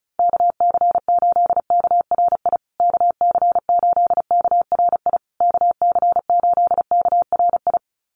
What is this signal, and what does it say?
Morse code, heard as a keyed tone, about 34 wpm, 700 Hz. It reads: KC8KRI KC8KRI KC8KRI
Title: USA Call Signs Repeated 3x 35wpm